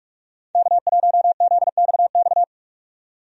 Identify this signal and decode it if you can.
Morse code, heard as a keyed tone, about 46 wpm, 700 Hz. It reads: K1ZXX
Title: USA Call Signs - 5 Characters - 45wpm